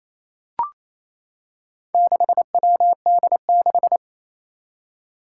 Morse code, heard as a keyed tone, about 27 wpm, 700 Hz. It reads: 6WD6
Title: Four Challenging Letter-Number Combinations 28wpm